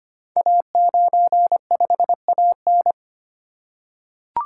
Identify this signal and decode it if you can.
Morse code, heard as a keyed tone, about 25 wpm, 690 Hz. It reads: A95AN
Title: Code Groups - 5 Letter:Numbers - 25wpm